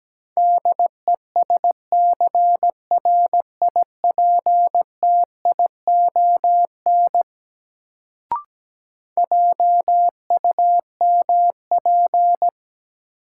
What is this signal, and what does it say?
Morse code, heard as a keyed tone, about 17 wpm, 700 Hz. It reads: DESCRIPTION JUMP
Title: Sentences from Top 1300 Words - New Word Drill Alt - 17wpm